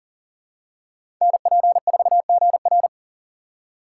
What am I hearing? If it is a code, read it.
Morse code, heard as a keyed tone, about 40 wpm, 700 Hz. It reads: NP4GR
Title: USA Call Signs - 2x Character Spacing - 40wpm